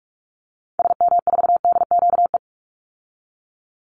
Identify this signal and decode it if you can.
Morse code, heard as a keyed tone, about 44 wpm, 700 Hz. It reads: SM4DQE
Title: Call Signs Repeated 3x - 45wpm